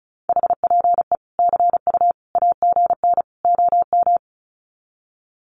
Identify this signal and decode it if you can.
Morse code, heard as a keyed tone, about 34 wpm, 700 Hz. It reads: HPE CU AGN OM
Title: Common QSO Phrases 35wpm